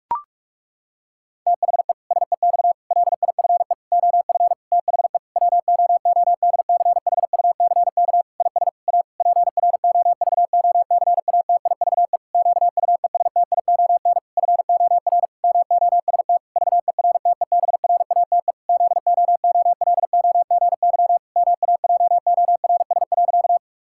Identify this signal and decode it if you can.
Morse code, heard as a keyed tone, about 41 wpm, 700 Hz. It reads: THE SEX LIFE OF THE WOODCHUCK IS A PROVOCATIVE QUESTION FOR MOST VERTEBRATE ZOOLOGY MAJORS.
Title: Pangram Challenge v2 - 45wpm